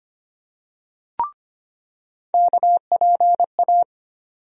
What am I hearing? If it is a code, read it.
Morse code, heard as a keyed tone, about 25 wpm, 700 Hz. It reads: KPA
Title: USA Call Sign Postfixes - 3 Letters 25wpm